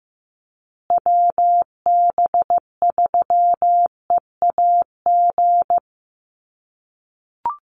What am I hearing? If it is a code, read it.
Morse code, heard as a keyed tone, about 15 wpm, 700 Hz. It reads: WB3EAG
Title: USA Call Signs Repeated 3x 15wpm